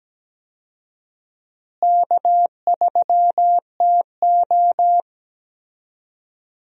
Morse code, heard as a keyed tone, about 17 wpm, 700 Hz. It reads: K3TO
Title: USA Call Signs - 4 Characters - Repeated 3x - 17wpm